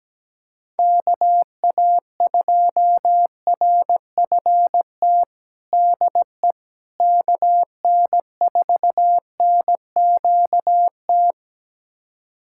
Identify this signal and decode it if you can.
Morse code, heard as a keyed tone, about 17 wpm, 700 Hz. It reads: KA2RFT DE KN4NQT
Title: USA Callsign DE Callsign 17wpm